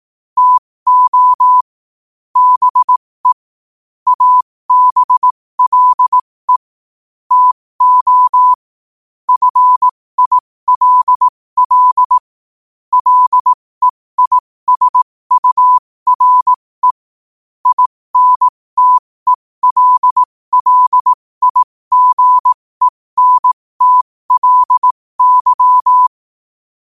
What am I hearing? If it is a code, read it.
Morse code, heard as a keyed tone, about 18 wpm, spaced as for 15 wpm, 1000 Hz. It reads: TO BE ABLE TO FILL LEISURE INTELLIGENTLY
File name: qotdfiles / 15wpm